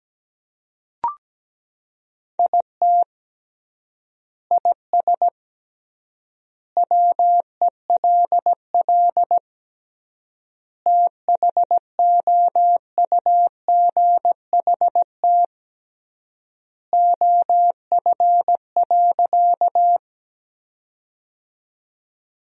Morse code, heard as a keyed tone, about 17 wpm, 700 Hz. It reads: IT IS WELL THOUGHT OF.
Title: 3x Word-Spacing - Sentences from Top 300 Words - 17wpm